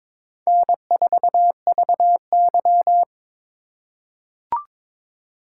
Morse code, heard as a keyed tone, about 21 wpm, 700 Hz. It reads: N4VY
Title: USA Call Signs - 4x Character Spacing - 22wpm